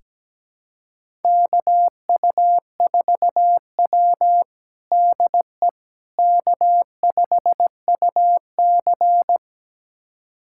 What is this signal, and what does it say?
Morse code, heard as a keyed tone, about 17 wpm, 700 Hz. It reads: KU4W DE K5UC
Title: USA Callsign DE Callsign - 4 Characters - 17wpm